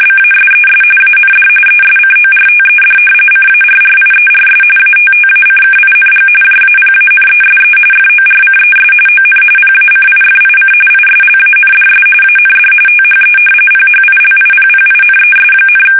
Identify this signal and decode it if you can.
RTTY, about 45.45 baud. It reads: BASES AND ESCORTED CARRIERS ATTACKING JAPANESE FORCES ON PACIFIC ISLANDS AND THE PHILIPPINES,
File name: fsk850 / battleshipnj